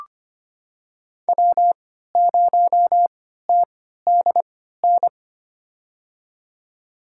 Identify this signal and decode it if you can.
Morse code, heard as a keyed tone, about 25 wpm, spaced as for 14 wpm, 700 Hz. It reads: W0TDN
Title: USA Call Signs - 5 Characters - 3x Character Spacing - 25wpm